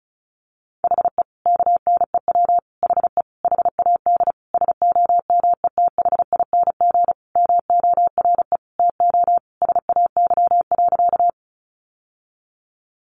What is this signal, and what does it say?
Morse code, heard as a keyed tone, about 34 wpm, 700 Hz. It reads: HE KNEW HE HAD SOMETHING MORE TO SAY.
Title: Sentences from Top 600 Words 35wpm